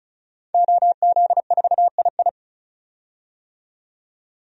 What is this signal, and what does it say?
Morse code, heard as a keyed tone, about 34 wpm, 700 Hz. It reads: OZ4II
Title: DXCC-100 Call Signs - 5 Characters - 35wpm